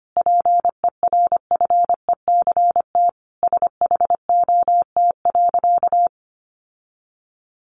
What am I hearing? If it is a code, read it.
Morse code, heard as a keyed tone, about 25 wpm, 700 Hz. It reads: PERFECT SHOT.
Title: Sentences from Top 1100 Words 25wpm